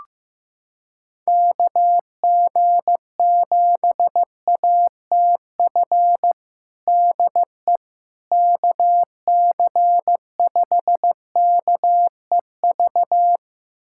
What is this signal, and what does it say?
Morse code, heard as a keyed tone, about 15 wpm, 700 Hz. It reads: KG7ATF DE KC5KEV
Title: USA Callsign DE Callsign - 6 Characters - 15wpm